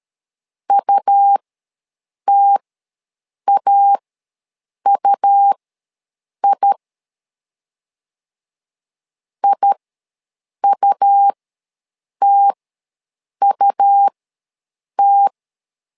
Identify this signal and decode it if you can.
Morse code, heard as a keyed tone, about 13 wpm, spaced as for 7 wpm, 800 Hz. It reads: UTAUI IUTUT
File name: LEZIONI / 2 UHDN / 2 U ws 12wpm 1655